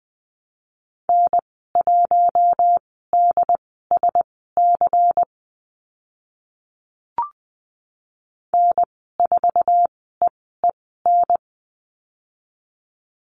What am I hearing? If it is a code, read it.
Morse code, heard as a keyed tone, about 20 wpm, spaced as for 14 wpm, 700 Hz. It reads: N1DSC N4EEN
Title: 140 Lesson 23g - Letter f - Call Signs 20wpm